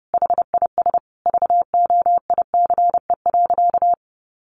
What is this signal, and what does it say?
Morse code, heard as a keyed tone, about 28 wpm, 700 Hz. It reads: HIS VOICE.
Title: Sentences from Top 500 Words 30wpm